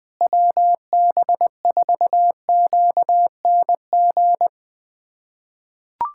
Morse code, heard as a keyed tone, about 21 wpm, 700 Hz. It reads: WB4QNG
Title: USA Callsign DE Callsign - 6 Characters - 20wpm